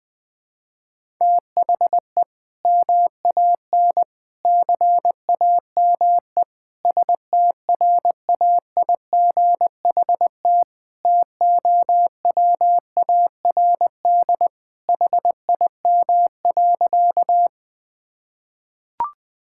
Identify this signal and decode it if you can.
Morse code, heard as a keyed tone, about 20 wpm, 700 Hz. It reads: THE MAN CAME STRAIGHT TOWARD HIM.
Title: Sentences from Top 1000 Words - Review Part 6 - 20wpm